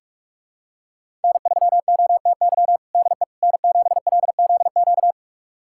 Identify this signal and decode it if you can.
Morse code, heard as a keyed tone, about 46 wpm, 700 Hz. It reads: N3OTY DE N7LZQ
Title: USA Callsign DE Callsign - 5 Characters - 45wpm